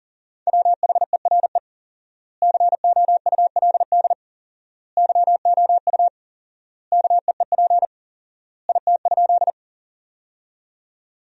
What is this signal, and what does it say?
Morse code, heard as a keyed tone, about 40 wpm, 700 Hz. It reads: WHERE COULD YOU KEEP IT?
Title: 4x Word-Spacing - Sentences from Top 300 Words - 40wpm